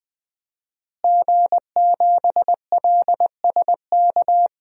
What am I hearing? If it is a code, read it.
Morse code, heard as a keyed tone, about 21 wpm, 700 Hz. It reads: G7LSK
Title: DXCC-100 Call Signs - 5 Characters - Repeated 2x - 20wpm